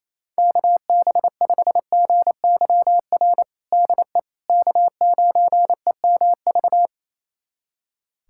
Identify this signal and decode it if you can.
Morse code, heard as a keyed tone, about 28 wpm, 700 Hz. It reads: KB5GYR DE K9EMV
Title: USA Callsign DE Callsign 28wpm